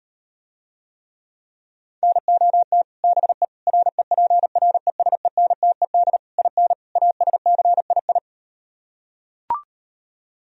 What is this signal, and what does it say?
Morse code, heard as a keyed tone, about 38 wpm, 700 Hz. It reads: NOT BE REPRESENTED IN ASCII